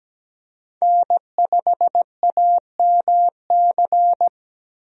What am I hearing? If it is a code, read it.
Morse code, heard as a keyed tone, about 17 wpm, 700 Hz. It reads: N5AMC
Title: USA Call Signs - 5 Characters - Repeated 2x - 17wpm